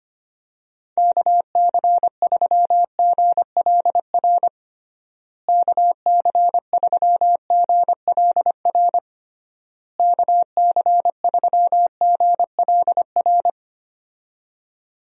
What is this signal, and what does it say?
Morse code, heard as a keyed tone, about 25 wpm, 700 Hz. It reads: KC3GLR KC3GLR KC3GLR
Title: USA Call Signs - 6 Characters - Repeated 3x - 25wpm